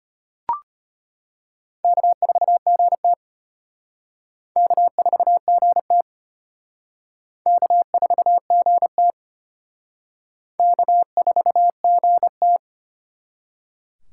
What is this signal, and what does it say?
Morse code, heard as a keyed tone, about 33 wpm, 700 Hz. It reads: K4GT K4GT K4GT K4GT